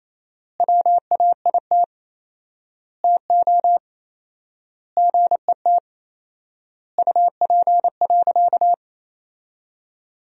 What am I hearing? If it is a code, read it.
Morse code, heard as a keyed tone, about 28 wpm, 700 Hz. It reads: WAIT TO GET UP.
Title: 4x Word-Spacing - Sentences from Top 500 Words - 28wpm